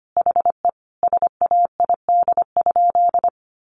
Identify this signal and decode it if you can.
Morse code, heard as a keyed tone, about 25 wpm, 700 Hz. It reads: HE SAID?
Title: Sentences from Top 900 Words 25wpm